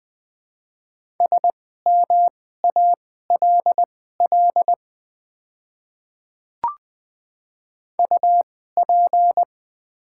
Morse code, heard as a keyed tone, about 20 wpm, spaced as for 14 wpm, 700 Hz. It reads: SMALL UP
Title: 2x Character Spacing - Top 200 Words - 20wpm